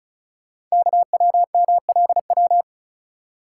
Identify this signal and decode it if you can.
Morse code, heard as a keyed tone, about 36 wpm, 690 Hz. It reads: KWMLW
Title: Code Groups - 5 Letters - 35wpm